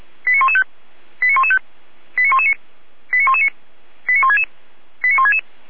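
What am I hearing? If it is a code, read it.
Five-tone selective call (ZVEI), heard as quick runs of five tones, tones of about 70 ms. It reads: 89115 89115 89119 89119 89166 89166
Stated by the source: ZVEI five-tone call 89097 received on SDR.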